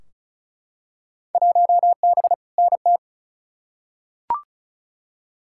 Morse code, heard as a keyed tone, about 36 wpm, 690 Hz. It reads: 1B NT
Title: Field Day Exchanges Encore - 35wpm